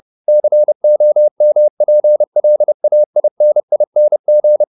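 Morse code, heard as a keyed tone, about 30 wpm, 600 Hz. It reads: COMPLAINING
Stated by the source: Morse code at 30 wpm, the novel Never by Ken Follett in English, from CW med Gnister.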